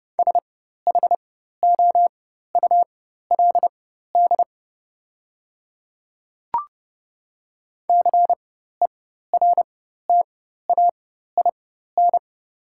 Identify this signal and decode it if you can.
Morse code, heard as a keyed tone, about 30 wpm, spaced as for 14 wpm, 690 Hz. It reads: SHOULD CERTAIN
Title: 4x Character Spacing - Top 500 Words - 30wpm